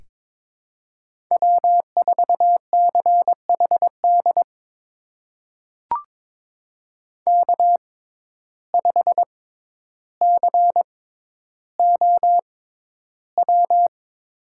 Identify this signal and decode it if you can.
Morse code, heard as a keyed tone, about 22 wpm, spaced as for 11 wpm, 700 Hz. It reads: W4CHD K5COW
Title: USA Call Signs - 5 Characters - 6x Character Spacing - 22wpm